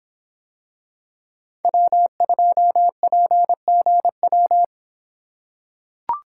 Morse code, heard as a keyed tone, about 24 wpm, 700 Hz. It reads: W2PGW